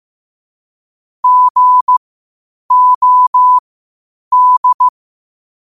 Morse code, heard as a keyed tone, about 15 wpm, spaced as for 8 wpm, 1000 Hz. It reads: GOD
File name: qotdfiles / 8wpm